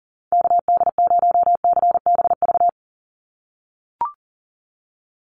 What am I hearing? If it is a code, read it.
Morse code, heard as a keyed tone, about 38 wpm, 700 Hz. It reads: KD0CBV
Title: USA Call Signs 40wpm